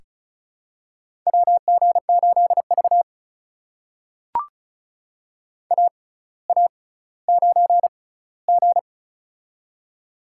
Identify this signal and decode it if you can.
Morse code, heard as a keyed tone, about 33 wpm, spaced as for 18 wpm, 700 Hz. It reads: WG8V AA9G
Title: USA Call Signs - 4 Characters - 6x Character Spacing - 35wpm